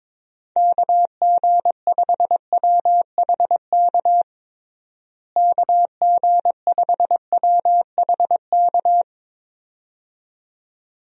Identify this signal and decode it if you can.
Morse code, heard as a keyed tone, about 23 wpm, 700 Hz. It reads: KG5WHK KG5WHK
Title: USA Call Signs - 6 Characters - Repeated 2x - 22wpm